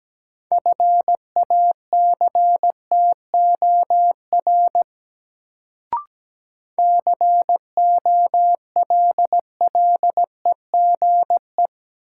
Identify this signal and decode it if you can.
Morse code, heard as a keyed tone, about 17 wpm, 700 Hz. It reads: FACTOR COLLEGE
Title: Sentences from Top 1300 Words - New Word Drill Alt - 17wpm